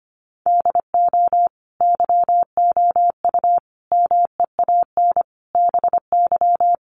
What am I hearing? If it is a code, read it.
Morse code, heard as a keyed tone, about 25 wpm, 700 Hz. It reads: DO YOU MEAN BY
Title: Sets of 5 Words Encore2 25wpm